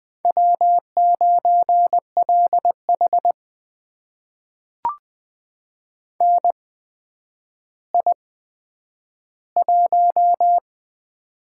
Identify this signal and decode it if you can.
Morse code, heard as a keyed tone, about 20 wpm, spaced as for 10 wpm, 700 Hz. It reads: W9LH NI1
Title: USA Call Signs - 4 Characters - 8x Character Spacing - 20wpm